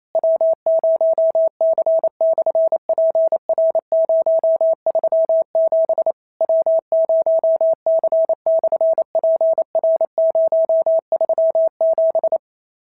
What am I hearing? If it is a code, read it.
Morse code, heard as a keyed tone, about 27 wpm, 650 Hz. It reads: W0C/PR037 W0C/PR037
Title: SOTA QSO Practice 28wpm